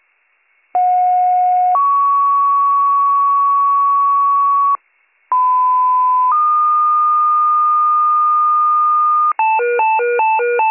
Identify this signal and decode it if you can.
Two-tone page, heard as a one-second tone, then a three-second tone, about 730 then 1120 Hz, about 1010 then 1220 Hz. It is heard from 0.7 to 9.3 s.